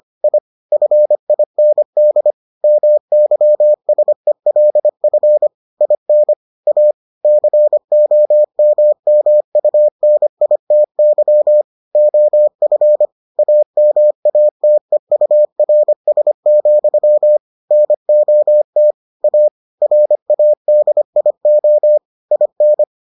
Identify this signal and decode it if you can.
Morse code, heard as a keyed tone, about 25 wpm, 600 Hz. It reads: I FIND MYSELF IN A COMMUNITY OF AMATEURS, NOT A RADIO IN